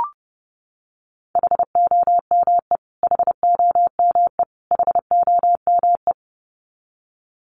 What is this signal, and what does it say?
Morse code, heard as a keyed tone, about 30 wpm, 700 Hz. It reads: HOME HOME HOME
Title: Top 200 Words Repeated 30wpm